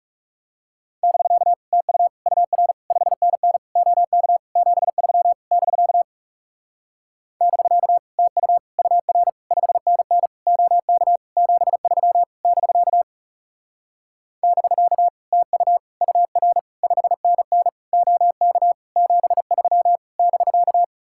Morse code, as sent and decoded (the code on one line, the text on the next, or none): -...-.- / - ..- / ..- .-. / ..... -. -. / --- -.- / --... ...-- / -...-.- / -...-.- / - ..- / ..- .-. / ..... -. -. / --- -.- / --... ...-- / -...-.- / -...-.- / - ..- / ..- .-. / ..... -. -. / --- -.- / --... ...-- / -...-.-
<BK> TU UR 5NN OK 73 <BK> <BK> TU UR 5NN OK 73 <BK> <BK> TU UR 5NN OK 73 <BK>